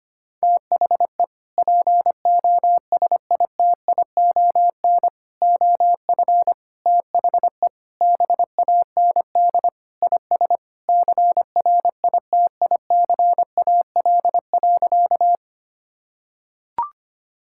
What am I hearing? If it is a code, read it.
Morse code, heard as a keyed tone, about 25 wpm, 700 Hz. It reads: THE POSITION OF THE BAND IS CRITICAL.
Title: Sentences from Top 1600 Words 25wpm